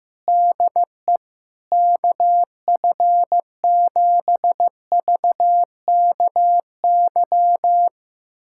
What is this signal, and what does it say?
Morse code, heard as a keyed tone, about 16 wpm, 700 Hz. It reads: DE KF7VKY
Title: USA Callsign DE Callsign 15wpm